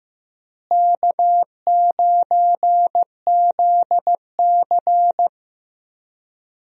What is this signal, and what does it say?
Morse code, heard as a keyed tone, about 15 wpm, 700 Hz. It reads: K9ZC
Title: USA Call Signs - 4 Characters - Repeated 3x - 15wpm